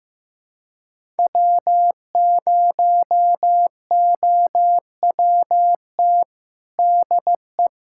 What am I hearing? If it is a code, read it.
Morse code, heard as a keyed tone, about 15 wpm, 700 Hz. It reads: W0OWT DE
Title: USA Callsign DE Callsign 15wpm